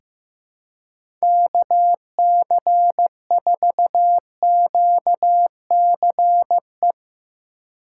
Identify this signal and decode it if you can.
Morse code, heard as a keyed tone, about 15 wpm, 700 Hz. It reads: KC4QCE
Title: USA Call Signs - 6 Characters - 4x Character Spacing - 15wpm